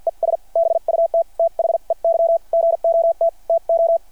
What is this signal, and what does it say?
Morse code, heard as a keyed tone, about 53 wpm, spaced as for 43 wpm, 690 Hz. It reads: ES BUT THEY GOT TO
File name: archived / CRSnet